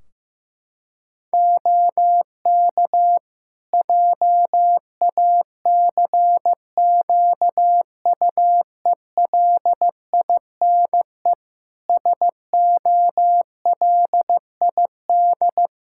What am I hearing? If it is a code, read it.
Morse code, heard as a keyed tone, about 16 wpm, 700 Hz. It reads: OK JACQUELINE SOLID
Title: K1USN SST QSO Practice 15wpm